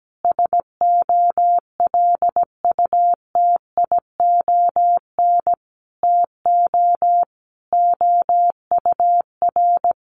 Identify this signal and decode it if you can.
Morse code, heard as a keyed tone, about 17 wpm, 700 Hz. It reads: SOLUTION TO OUR
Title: Sentences from Top 1300 Words 17wpm